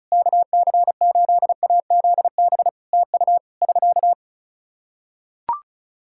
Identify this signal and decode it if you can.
Morse code, heard as a keyed tone, about 36 wpm, 700 Hz. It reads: KC8AZB TU <SK>
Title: Practice QSO - Exchange 4 - Ending - 35wpm